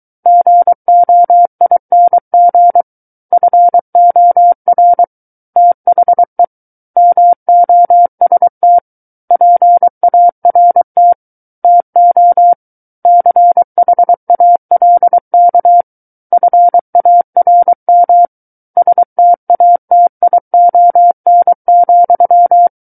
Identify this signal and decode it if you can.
Morse code, heard as a keyed tone, about 23 wpm, 700 Hz. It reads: GOING FOR THE MOST PART TO CHALK FARM STATION,
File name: Book One / 14-Chapter 14 - 23 WPM